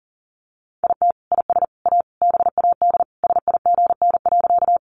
Morse code, heard as a keyed tone, about 40 wpm, 700 Hz. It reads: IT IS A BAD SIGN.
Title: Sentences from Top 1000 Words 40wpm